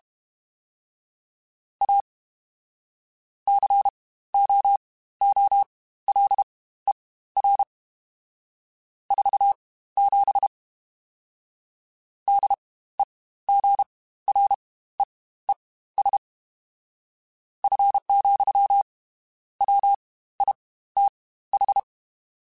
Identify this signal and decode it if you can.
Morse code, heard as a keyed tone, about 32 wpm, spaced as for 15 wpm, 800 Hz. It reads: A COOLER 47 DEGREES F, WITH
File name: Spaced Characters / Spaced Characters-mp3s / Weather 5 5 sp @30